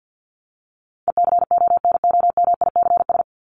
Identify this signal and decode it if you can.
Morse code, heard as a keyed tone, about 50 wpm, 700 Hz. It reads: ECONOMICS
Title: Top 501-750 Words 50wpm